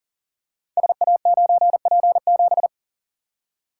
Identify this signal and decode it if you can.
Morse code, heard as a keyed tone, about 42 wpm, 690 Hz. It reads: SA9P7
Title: Code Groups - 5 Letter:Numbers - 40wpm